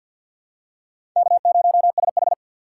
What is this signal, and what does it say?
Morse code, heard as a keyed tone, about 51 wpm, 700 Hz. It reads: K0SH
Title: USA Call Signs - 3x Character Spacing - 50wpm